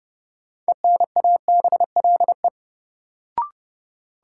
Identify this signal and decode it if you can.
Morse code, heard as a keyed tone, about 30 wpm, 700 Hz. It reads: ENABLE
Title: Sentences from Top 1500 Words - New Word Drill - 30wpm